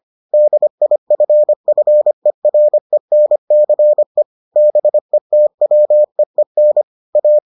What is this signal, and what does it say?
Morse code, heard as a keyed tone, about 25 wpm, 600 Hz. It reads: DIFFERENCE BETWEEN A